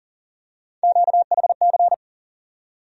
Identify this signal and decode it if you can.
Morse code, heard as a keyed tone, about 40 wpm, 700 Hz. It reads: QHC
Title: USA Call Sign Postfixes - 3 Letters 40wpm